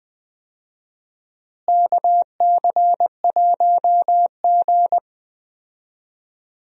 Morse code, heard as a keyed tone, about 20 wpm, 700 Hz. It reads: KC1G
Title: USA Call Signs - 4 Characters - Repeated 3x - 20wpm